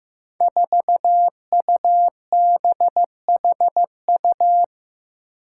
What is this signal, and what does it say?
Morse code, heard as a keyed tone, about 15 wpm, 700 Hz. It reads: 4UBHU
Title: Five Challenging Letter-Number Combinations 15wpm